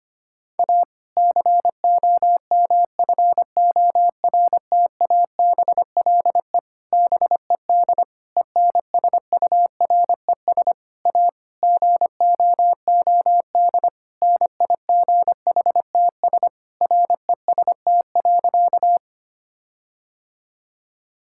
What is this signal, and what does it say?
Morse code, heard as a keyed tone, about 25 wpm, 700 Hz. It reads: A COMFORTABLE BED ENSURES A GOOD NIGHTS REST.
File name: practice / 25wpm / Difficult Sentences with Hints - 25wpm